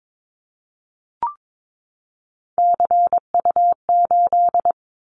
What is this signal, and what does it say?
Morse code, heard as a keyed tone, about 22 wpm, 700 Hz. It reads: CU8
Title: Callsign Prefixes for Top 100 DXCC - 22wpm